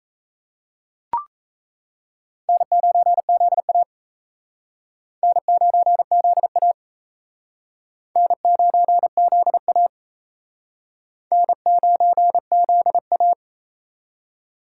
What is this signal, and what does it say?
Morse code, heard as a keyed tone, about 35 wpm, 700 Hz. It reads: N9ZA N9ZA N9ZA N9ZA